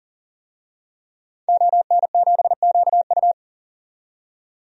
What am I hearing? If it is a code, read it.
Morse code, heard as a keyed tone, about 40 wpm, 700 Hz. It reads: ON7QU
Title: DXCC-100 Call Signs - 5 Characters - 40wpm